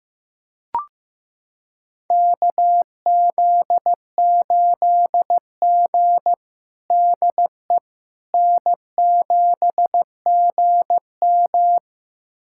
Morse code, heard as a keyed tone, about 15 wpm, 700 Hz. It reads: KZ8G DE N7GM
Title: USA Callsign DE Callsign - 4 Characters - 15wpm